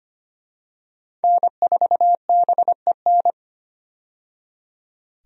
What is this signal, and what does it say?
Morse code, heard as a keyed tone, about 25 wpm, 700 Hz. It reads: N4BEN
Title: USA Call Signs - 5 Characters - 25wpm